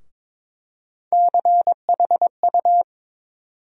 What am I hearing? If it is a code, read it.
Morse code, heard as a keyed tone, about 22 wpm, 700 Hz. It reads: CHU
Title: USA Call Sign Postfixes - 3 Letters 22wpm